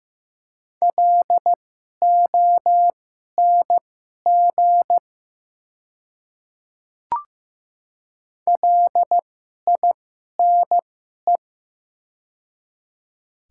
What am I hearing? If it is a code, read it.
Morse code, heard as a keyed tone, about 15 wpm, spaced as for 11 wpm, 700 Hz. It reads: LONG LINE
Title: 2x Character Spacing - Top 200 Words - 15wpm